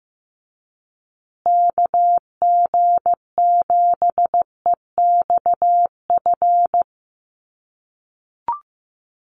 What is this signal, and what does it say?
Morse code, heard as a keyed tone, about 15 wpm, 700 Hz. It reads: KG7EXF
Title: USA Call Signs Repeated 3x 15wpm